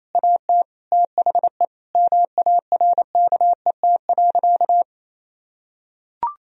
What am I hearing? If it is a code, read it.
Morse code, heard as a keyed tone, about 28 wpm, 700 Hz. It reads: AT THE MARKET.
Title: Sentences from Top 1000 Words - Review Part 3 - 28wpm